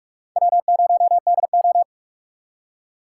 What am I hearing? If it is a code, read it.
Morse code, heard as a keyed tone, about 45 wpm, 690 Hz. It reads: W0DO
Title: USA Call Signs - 8x Character Spacing - 45wpm